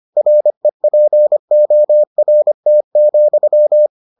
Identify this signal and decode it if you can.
Morse code, heard as a keyed tone, about 25 wpm, 600 Hz. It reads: REPORT,